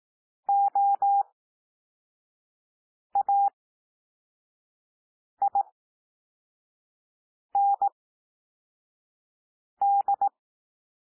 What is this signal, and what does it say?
Morse code, heard as a keyed tone, about 18 wpm, spaced as for 4 wpm, 800 Hz. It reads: OAIND
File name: cwmp3 / L2 SOUND LETTERS 18 wpm